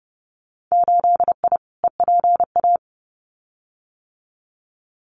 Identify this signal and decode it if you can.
Morse code, heard as a keyed tone, about 30 wpm, 700 Hz. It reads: 8I EPA
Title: Winter Field Day Exchanges Encore 30wpm